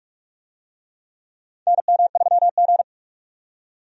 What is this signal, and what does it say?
Morse code, heard as a keyed tone, about 46 wpm, 700 Hz. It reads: NM3G
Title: USA Call Signs - 4 Characters - 45wpm